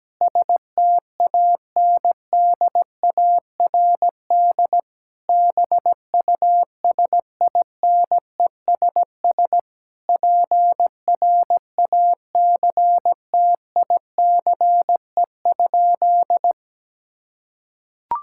Morse code, heard as a keyed tone, about 17 wpm, 700 Hz. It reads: STANDARD BUSINESS PRACTICE?
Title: Sentences from Top 1000 Words - Review Part 6 - 17wpm